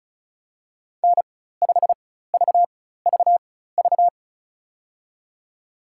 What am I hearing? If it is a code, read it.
Morse code, heard as a keyed tone, about 36 wpm, spaced as for 16 wpm, 700 Hz. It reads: N5VVV
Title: USA Call Signs - 5 Characters - 4x Character Spacing - 35wpm